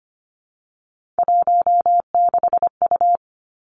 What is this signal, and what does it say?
Morse code, heard as a keyed tone, about 25 wpm, 700 Hz. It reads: 16U
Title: Three Challenging Letter-Number Combinations 25wpm